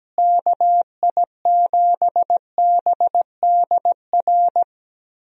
Morse code, heard as a keyed tone, about 17 wpm, 700 Hz. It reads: KI7BDR
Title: USA Call Signs - 6 Characters - Repeated 3x - 17wpm